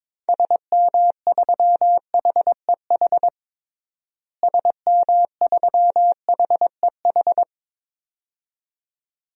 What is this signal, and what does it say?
Morse code, heard as a keyed tone, about 22 wpm, 700 Hz. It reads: SM3HEH SM3HEH
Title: DXCC-100 Call Signs - 6 Characters - Repeated 2x - 22wpm